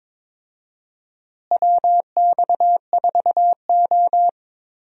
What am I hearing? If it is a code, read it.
Morse code, heard as a keyed tone, about 22 wpm, 700 Hz. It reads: WX4O
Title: USA Call Signs - 4 Characters - Repeated 2x - 22wpm